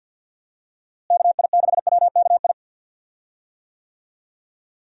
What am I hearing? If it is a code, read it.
Morse code, heard as a keyed tone, about 51 wpm, 700 Hz. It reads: KI6WKI
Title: USA Call Signs - 6 Characters - Repeated 3x - 50wpm